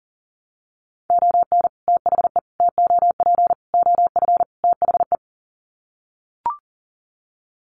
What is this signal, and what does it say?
Morse code, heard as a keyed tone, about 40 wpm, 700 Hz. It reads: ON THE TOP OF THE
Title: Sets of 5 Words Encore 40wpm